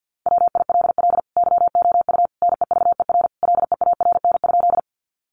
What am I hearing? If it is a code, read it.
Morse code, heard as a keyed tone, about 50 wpm, 700 Hz. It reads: WILL YOU NEVER LEARN?
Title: Sentences from Top 300 Words 50wpm